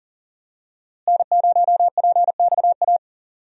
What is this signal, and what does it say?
Morse code, heard as a keyed tone, about 40 wpm, 700 Hz. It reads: N0PXA
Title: USA Call Signs - 5 Characters - Repeated 3x - 40wpm